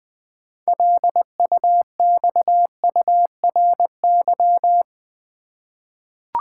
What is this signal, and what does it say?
Morse code, heard as a keyed tone, about 20 wpm, 700 Hz. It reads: LUXURY
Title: 6 Letter Words - First 3 Letters Given - 20wpm